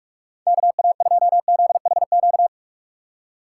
Morse code, heard as a keyed tone, about 44 wpm, 700 Hz. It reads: KA2ZHQ
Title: USA Call Signs - 3x Character Spacing - 45wpm